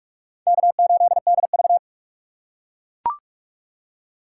Morse code, heard as a keyed tone, about 40 wpm, 700 Hz. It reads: K8DV